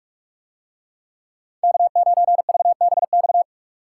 Morse code, heard as a keyed tone, about 44 wpm, 700 Hz. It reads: K9VBX